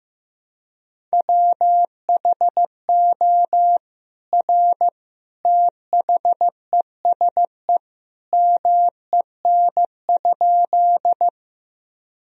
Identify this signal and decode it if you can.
Morse code, heard as a keyed tone, about 15 wpm, 700 Hz. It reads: WHO R THESE MEN?
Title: Sentences from Top 200 Words and Abbreviations 15wpm